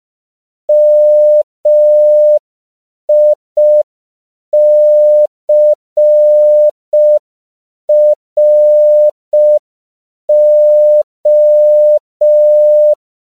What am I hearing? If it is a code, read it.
Morse code, heard as a keyed tone, about 5 wpm, 600 Hz. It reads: MICRO
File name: CW / 5PPM